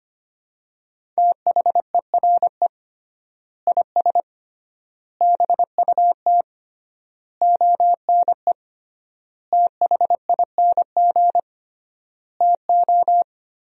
Morse code, heard as a keyed tone, about 25 wpm, 700 Hz. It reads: THERE IS BUT ONE THING TO
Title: 3x Word-Spacing - Sentences from Top 100 Words - 25wpm